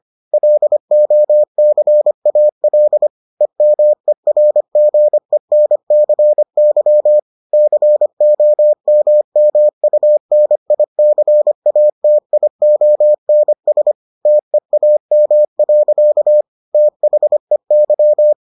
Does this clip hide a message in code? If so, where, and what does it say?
Yes: Morse code from 0.3 to 18.3 s, heard as a keyed tone, about 25 wpm, 600 Hz. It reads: LOCAL EMERGENCY COMMUNICATIONS TEAM. THEY